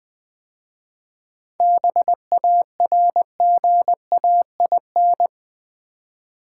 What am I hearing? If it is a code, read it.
Morse code, heard as a keyed tone, about 20 wpm, 700 Hz. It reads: BARGAIN
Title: 7 Letter Words - First 3 Letters Given - 20wpm